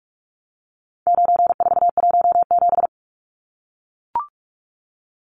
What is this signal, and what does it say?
Morse code, heard as a keyed tone, about 44 wpm, 700 Hz. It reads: 9417
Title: Numbers - 4-Digit 45wpm